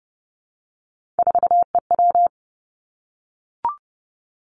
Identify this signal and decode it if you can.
Morse code, heard as a keyed tone, about 30 wpm, 700 Hz. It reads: 4EW
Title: Three Challenging Letter-Number Combinations 30wpm